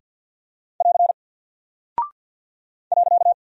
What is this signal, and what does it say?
Morse code, heard as a keyed tone, about 48 wpm, 700 Hz. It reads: <AR> .
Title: Single Character (Letter:Number:Punctuation:Prosign) Alt 50wpm